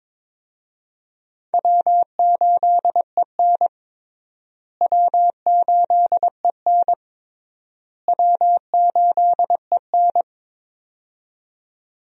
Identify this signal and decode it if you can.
Morse code, heard as a keyed tone, about 22 wpm, 700 Hz. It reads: W8EN W8EN W8EN
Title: USA Call Signs - 4 Characters - Repeated 3x - 22wpm